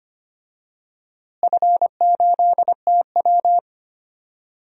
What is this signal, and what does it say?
Morse code, heard as a keyed tone, about 25 wpm, 700 Hz. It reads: F8TW
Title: DXCC-100 Call Signs - 4 Characters - Repeated 2x - 25wpm